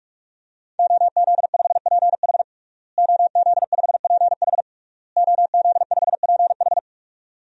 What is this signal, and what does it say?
Morse code, heard as a keyed tone, about 46 wpm, 700 Hz. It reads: OZ5PH OZ5PH OZ5PH
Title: DXCC-100 Call Signs - 5 Characters - Repeated 3x - 45wpm